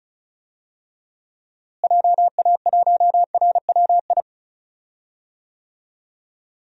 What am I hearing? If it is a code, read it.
Morse code, heard as a keyed tone, about 36 wpm, 700 Hz. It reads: JA1RWI
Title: USA Call Signs - 6 Characters - Repeated 2x - 35wpm